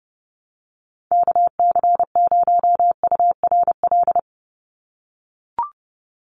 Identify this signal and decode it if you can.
Morse code, heard as a keyed tone, about 29 wpm, 700 Hz. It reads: KC0URL
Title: Call Signs - 30wpm